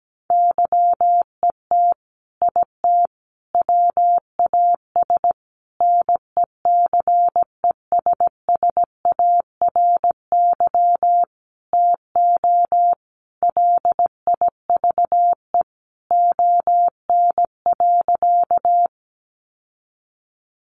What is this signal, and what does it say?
Morse code, heard as a keyed tone, about 17 wpm, 700 Hz. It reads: YET IT WAS NECESSARY TO LIVE ON.
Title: Sentences from Top 800 Words 17wpm